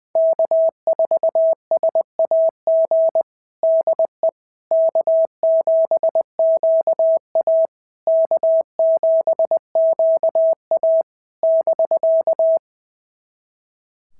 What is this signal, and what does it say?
Morse code, heard as a keyed tone, about 20 wpm, 650 Hz. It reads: K4SAG DE K7QA K7QA <BK>
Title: SOTA QSO Practice 20wpm